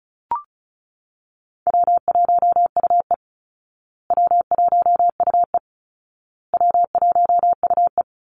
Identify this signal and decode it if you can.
Morse code, heard as a keyed tone, about 34 wpm, 700 Hz. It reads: W1UE W1UE W1UE
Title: Easy Call Signs Repeated 3x - 35wpm